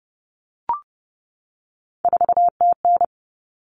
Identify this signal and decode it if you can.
Morse code, heard as a keyed tone, about 30 wpm, 700 Hz. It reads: 4TN
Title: Common Cut Numbers - 3-Digit 30wpm